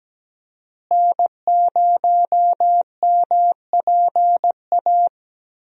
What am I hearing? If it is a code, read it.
Morse code, heard as a keyed tone, about 17 wpm, 700 Hz. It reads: N0MPA
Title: USA Call Signs - 5 Characters - Repeated 2x - 17wpm